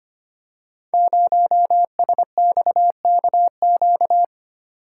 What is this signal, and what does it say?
Morse code, heard as a keyed tone, about 25 wpm, 690 Hz. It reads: 0SXKQ
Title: Code Groups - 5 Letter:Numbers - 25wpm